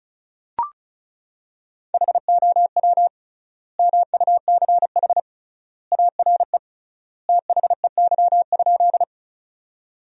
Morse code, heard as a keyed tone, about 34 wpm, 700 Hz. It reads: HOW MUCH ARE THEY?
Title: 3x Word-Spacing - Sentences from Top 200 Words - 35wpm